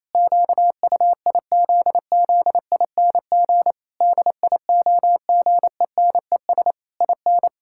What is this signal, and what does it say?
Morse code, heard as a keyed tone, about 28 wpm, 700 Hz. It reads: QUIZZING DIOGENES IN
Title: Pangram Challenge v2 - 28wpm